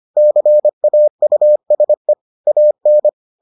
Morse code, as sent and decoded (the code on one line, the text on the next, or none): -.-. .- ..- ... / .- -.
CAUSE AN